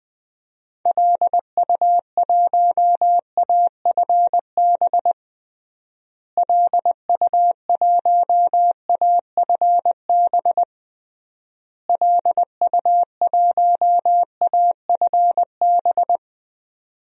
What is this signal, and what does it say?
Morse code, heard as a keyed tone, about 20 wpm, 700 Hz. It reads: LU1AFB LU1AFB LU1AFB
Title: DXCC-100 Call Signs - 6 Characters - Repeated 3x - 20wpm